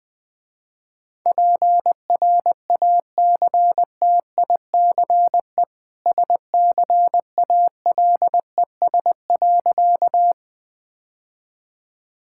Morse code, heard as a keyed tone, about 21 wpm, 700 Hz. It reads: PRACTICE SCALES.
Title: Easy Sentences with Hints - 20wpm